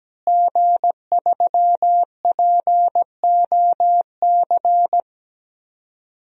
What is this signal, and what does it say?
Morse code, heard as a keyed tone, about 17 wpm, 700 Hz. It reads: G3POC
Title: DXCC-100 Call Signs - 5 Characters - Repeated 3x - 17wpm